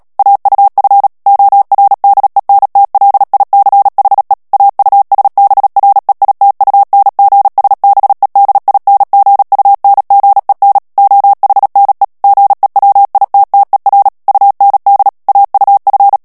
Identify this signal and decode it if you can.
Morse code, heard as a keyed tone, about 37 wpm, 800 Hz. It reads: AUF ORDENTLICHE AUSBREITUNGSBEDINGUNGEN OHNE GEWITTER UND AUF
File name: morse mp3 / ALLEMAND texte clair 2 vitesse 35 mots minute